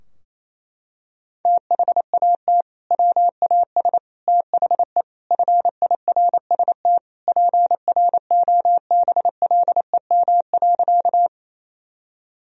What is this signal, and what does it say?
Morse code, heard as a keyed tone, about 28 wpm, 700 Hz. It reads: THAT WAS THE FIRST PROBLEM.
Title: Sentences from Top 1000 Words - Review Part 2 - 28wpm